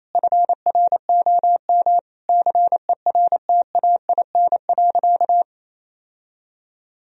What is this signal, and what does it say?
Morse code, heard as a keyed tone, about 28 wpm, 700 Hz. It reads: FROM CERTAIN.
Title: Sentences from Top 1000 Words - Review Part 2 - 28wpm